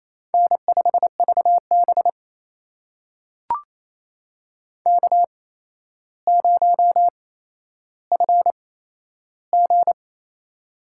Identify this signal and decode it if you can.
Morse code, heard as a keyed tone, about 28 wpm, spaced as for 12 wpm, 700 Hz. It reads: N5VB K0FG
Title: USA Call Signs - 4 Characters - 8x Character Spacing - 28wpm